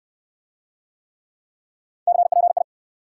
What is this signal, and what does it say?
Morse code, heard as a keyed tone, about 67 wpm, 700 Hz. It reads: CPI